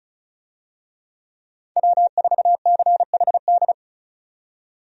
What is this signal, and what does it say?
Morse code, heard as a keyed tone, about 36 wpm, 700 Hz. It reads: W4CHD
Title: USA Call Signs - 5 Characters - 35wpm